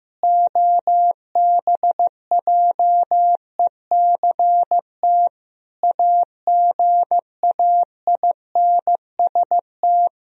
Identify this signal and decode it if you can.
Morse code, heard as a keyed tone, about 15 wpm, 700 Hz. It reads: OBJECT AGAINST
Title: Sentences from Top 1000 Words - Review Part 3 - 15wpm